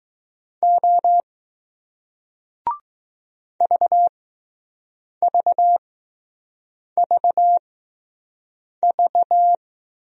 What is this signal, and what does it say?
Morse code, heard as a keyed tone, about 20 wpm, spaced as for 6 wpm, 700 Hz. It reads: O VVVV